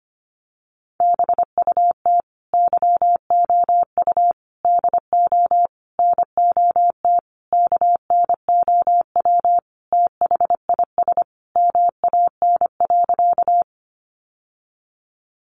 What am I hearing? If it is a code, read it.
Morse code, heard as a keyed tone, about 25 wpm, 700 Hz. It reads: BUT YOU DO NOT KNOW THIS MAN.
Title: Sentences from Top 600 Words 25wpm